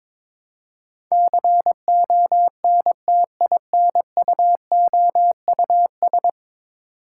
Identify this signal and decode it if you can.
Morse code, heard as a keyed tone, about 22 wpm, 700 Hz. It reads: CONTINUOUS
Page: morse-code-ninja-practice-22wpm